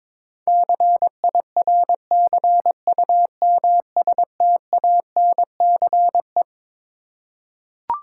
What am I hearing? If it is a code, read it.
Morse code, heard as a keyed tone, about 22 wpm, 700 Hz. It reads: CIRCUMSTANCE
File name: Twelve Letter Words - 22wpm